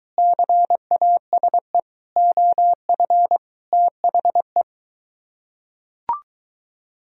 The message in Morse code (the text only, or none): CASE OF THE